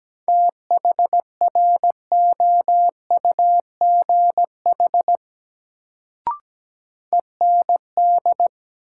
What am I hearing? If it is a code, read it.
Morse code, heard as a keyed tone, about 17 wpm, 700 Hz. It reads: THROUGH END
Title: Sets of 2 Words - New Word Drill - 17wpm